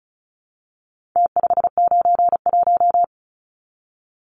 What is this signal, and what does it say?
Morse code, heard as a keyed tone, about 34 wpm, 700 Hz. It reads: T591
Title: Numbers - Contest Style - 4-Digit 35wpm